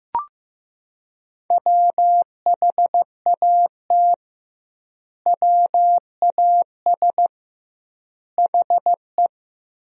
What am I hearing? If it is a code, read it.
Morse code, heard as a keyed tone, about 15 wpm, 700 Hz. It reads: WHAT WAS HE
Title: 2x Word-Spacing - Sentences from Top 100 Words - 15wpm